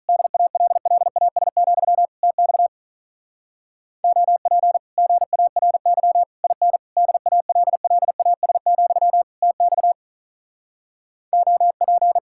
Morse code, heard as a keyed tone, about 48 wpm, 700 Hz. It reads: DALLAS, TX OP GARY IN DALLAS, TX OP